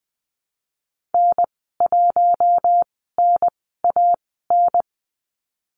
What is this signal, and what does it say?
Morse code, heard as a keyed tone, about 20 wpm, spaced as for 15 wpm, 700 Hz. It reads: N1NAN
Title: 076 Lesson 13g - Letter L - Call Signs 20wpm